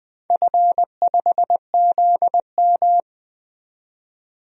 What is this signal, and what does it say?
Morse code, heard as a keyed tone, about 20 wpm, 700 Hz. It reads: F5ZM
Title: DXCC-100 Call Signs - 4 Characters - 20wpm